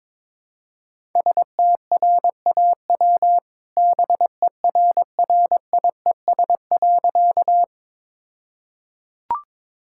Morse code, heard as a keyed tone, about 22 wpm, 700 Hz. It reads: STRAW BERRIES.
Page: morse-code-ninja-practice-22wpm